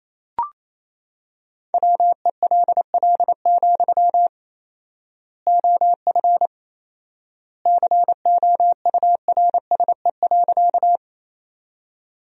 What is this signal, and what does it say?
Morse code, heard as a keyed tone, about 28 wpm, 700 Hz. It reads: WELL, OF COURSE.
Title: 4x Word-Spacing - Sentences from Top 500 Words - 28wpm